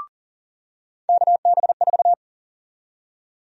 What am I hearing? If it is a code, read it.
Morse code, heard as a keyed tone, about 40 wpm, 700 Hz. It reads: KB4
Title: USA Call Sign Prefixes - 40wpm